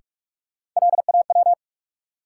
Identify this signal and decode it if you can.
Morse code, heard as a keyed tone, about 46 wpm, 700 Hz. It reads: LAW
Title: Sentences from Top 600 Words - New Word Drill - 45wpm